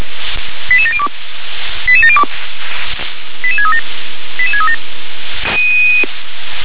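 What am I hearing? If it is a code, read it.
Five-tone selective call (ZVEI), heard as quick runs of five tones, tones of about 70 ms. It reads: 88702 88702 88637 88637
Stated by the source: ZVEI five-tone selective call 88702, received on SDR.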